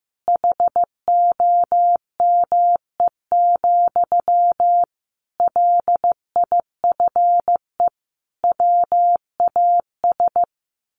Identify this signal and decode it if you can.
Morse code, heard as a keyed tone, about 15 wpm, 700 Hz. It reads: HOME, LIFE WAS
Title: Sentences from Top 1400 Words 15wpm